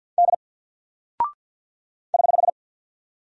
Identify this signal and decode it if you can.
Morse code, heard as a keyed tone, about 50 wpm, 700 Hz. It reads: D <HH>
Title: Single Character (Letter:Number:Punctuation:Prosign) 50wpm